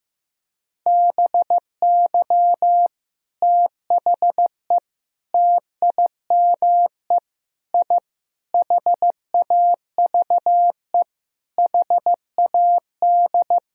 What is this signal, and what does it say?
Morse code, heard as a keyed tone, about 15 wpm, 700 Hz. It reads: BY THE TIME I HAVE HAD
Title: Sentences from Top 1000 Words - Review Part 1 - 15wpm